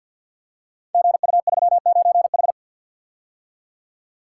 Morse code, heard as a keyed tone, about 47 wpm, 700 Hz. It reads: GU39H
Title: Code Groups - 5 Letter:Numbers - 50wpm